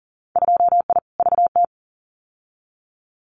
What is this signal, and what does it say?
Morse code, heard as a keyed tone, about 38 wpm, 700 Hz. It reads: 2I VT
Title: Winter Field Day Exchanges 40wpm